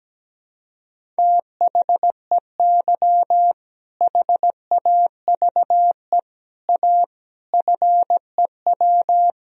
Morse code, heard as a keyed tone, about 17 wpm, 700 Hz. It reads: THEY HAVE A FEW
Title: Sentences from Top 1000 Words - Review Part 1 - 17wpm